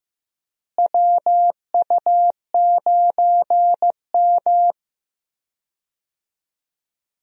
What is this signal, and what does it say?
Morse code, heard as a keyed tone, about 15 wpm, 700 Hz. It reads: WU9M
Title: USA Call Signs - 4 Characters - Repeated 2x - 15wpm